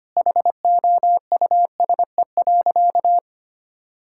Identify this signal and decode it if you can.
Morse code, heard as a keyed tone, about 25 wpm, 700 Hz. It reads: HOUSE.
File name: practice / 25wpm / Sentences with 4-Word Phrases - 25wpm